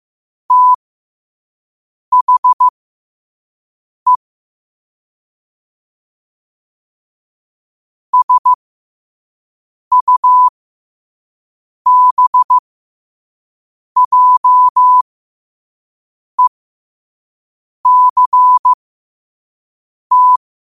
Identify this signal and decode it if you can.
Morse code, heard as a keyed tone, about 15 wpm, spaced as for 5 wpm, 1000 Hz. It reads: THE SUBJECT
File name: qotdfiles / 5wpm